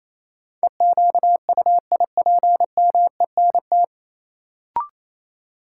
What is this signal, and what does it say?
Morse code, heard as a keyed tone, about 28 wpm, 700 Hz. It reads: EQUIPMENT
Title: Sentences from Top 700 Words - New Word Drill - 28wpm